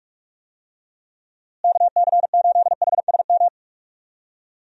Morse code, heard as a keyed tone, about 41 wpm, 700 Hz. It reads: KC8HSM
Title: USA Call Signs - 6 Characters - Repeated 2x - 45wpm